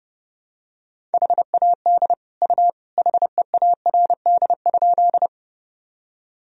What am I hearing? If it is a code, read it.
Morse code, heard as a keyed tone, about 30 wpm, 700 Hz. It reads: HAD U HEARD?
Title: Sentences from Top 400 Words and Abbreviations 30wpm